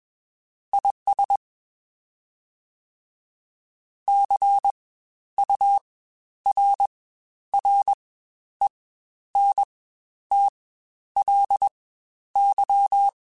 Morse code, heard as a keyed tone, about 21 wpm, spaced as for 11 wpm, 800 Hz. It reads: IS CURRENTLY
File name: Spaced Characters / Spaced Characters-mp3s / Weather5 22 sp @20